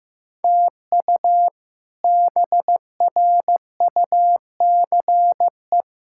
Morse code, heard as a keyed tone, about 15 wpm, 700 Hz. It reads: TU BRUCE
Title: K1USN SST QSO Practice - 15wpm